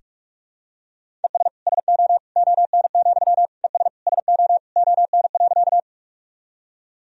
Morse code, heard as a keyed tone, about 46 wpm, 700 Hz. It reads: ES SO ON, ES SO ON.
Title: Sentences from Top 100 Words and Abbreviations 45wpm